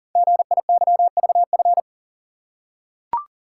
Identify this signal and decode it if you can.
Morse code, heard as a keyed tone, about 42 wpm, 700 Hz. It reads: GIYVF
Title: Code Groups - 5 Letters - 40wpm